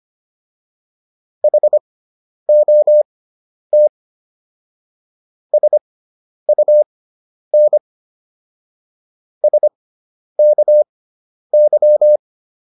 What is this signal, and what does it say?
Morse code, heard as a keyed tone, about 25 wpm, spaced as for 10 wpm, 600 Hz. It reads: HOT SUN SKY